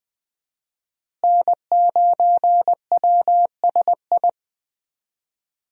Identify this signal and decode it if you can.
Morse code, heard as a keyed tone, about 20 wpm, 700 Hz. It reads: N9WSI
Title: USA Call Signs - 5 Characters - Repeated 3x - 20wpm